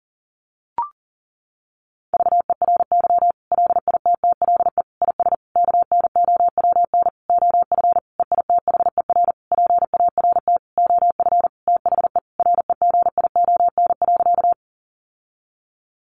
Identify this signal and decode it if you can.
Morse code, heard as a keyed tone, about 40 wpm, 700 Hz. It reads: VERY LITTLE IS KNOWN OF EITHER PART OF THE REGION.
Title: Sentences from Top 1500 Words 40wpm